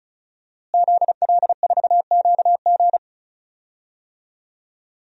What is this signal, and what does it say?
Morse code, heard as a keyed tone, about 36 wpm, 700 Hz. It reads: ZL4QG
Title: DXCC-100 Call Signs - 5 Characters - Repeated 2x - 35wpm